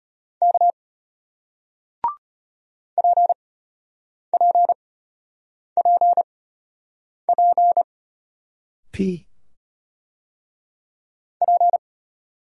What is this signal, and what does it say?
Morse code, heard as a keyed tone, about 34 wpm, spaced as for 8 wpm, 700 Hz. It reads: K PPPP P